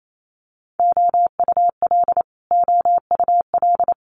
Morse code, heard as a keyed tone, about 28 wpm, 700 Hz. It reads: OUL OUL
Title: Multi-letter Phonemes Repeated 3x 28wpm